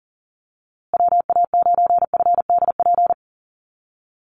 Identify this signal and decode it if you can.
Morse code, heard as a keyed tone, about 40 wpm, 700 Hz. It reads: WA9FDP
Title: USA Call Signs Repeated 2x 40wpm